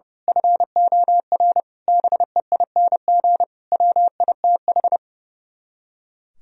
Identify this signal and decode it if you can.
Morse code, heard as a keyed tone, about 28 wpm, 700 Hz. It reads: FOR BEING WITH
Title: Speed-Racing - Sets of 5 Words 30wpm